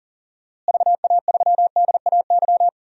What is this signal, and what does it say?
Morse code, heard as a keyed tone, about 37 wpm, 700 Hz. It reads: VA3DAY
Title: DXCC-100 Call Signs - 6 Characters - Repeated 3x - 40wpm